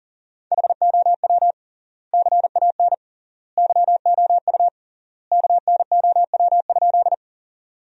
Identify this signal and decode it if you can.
Morse code, heard as a keyed tone, about 40 wpm, 700 Hz. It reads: HOW CAN YOU KNOW?
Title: 3x Word-Spacing - Sentences from Top 100 Words - 40wpm